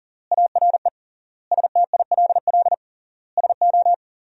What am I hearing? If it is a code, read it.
Morse code, heard as a keyed tone, about 40 wpm, 700 Hz. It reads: ARE STILL SO
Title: 3x Word-Spacing - Sentences from Top 400 Words - 40wpm